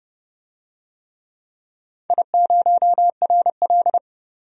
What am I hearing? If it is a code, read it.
Morse code, heard as a keyed tone, about 32 wpm, 700 Hz. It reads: I0RL
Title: DXCC-100 Call Signs - 4 Characters - 30wpm